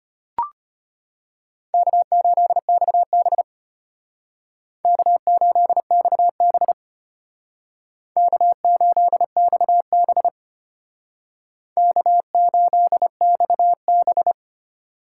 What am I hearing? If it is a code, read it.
Morse code, heard as a keyed tone, about 33 wpm, 700 Hz. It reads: K8XB K8XB K8XB K8XB